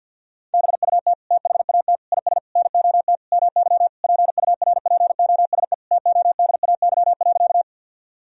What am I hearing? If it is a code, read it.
Morse code, heard as a keyed tone, about 50 wpm, 700 Hz. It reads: BUT THAT IS NOT MY PURPOSE TODAY.
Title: Sentences from Top 1000 Words - Review Part 4 - 50wpm